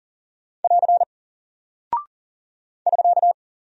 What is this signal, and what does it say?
Morse code, heard as a keyed tone, about 40 wpm, 700 Hz. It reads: <AR> <SK>
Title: Single Character (Letter:Number:Punctuation:Prosign) Alt 40wpm